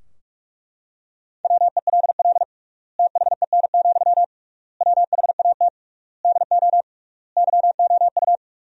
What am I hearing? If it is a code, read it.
Morse code, heard as a keyed tone, about 41 wpm, 700 Hz. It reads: WELL THEN, WHAT DO YOU
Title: 3x Word-Spacing - Sentences from Top 200 Words - 45wpm